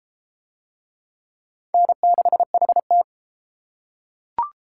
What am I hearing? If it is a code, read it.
Morse code, heard as a keyed tone, about 29 wpm, 700 Hz. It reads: N6HT